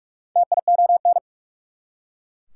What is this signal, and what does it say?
Morse code, heard as a keyed tone, about 44 wpm, 690 Hz. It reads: TION
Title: Prefixes and Suffixes 45wpm